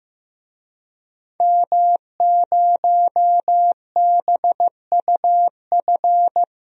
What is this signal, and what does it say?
Morse code, heard as a keyed tone, about 15 wpm, 700 Hz. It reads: M0BUF
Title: DXCC-100 Call Signs - 5 Characters - 15wpm